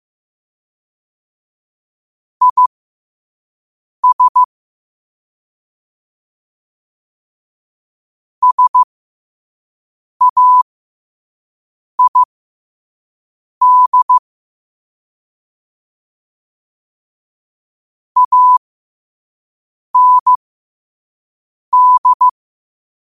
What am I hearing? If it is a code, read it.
Morse code, heard as a keyed tone, about 15 wpm, spaced as for 5 wpm, 1000 Hz. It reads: IS SAID AND
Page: New quotes every day in morse code at 5 Words per minute.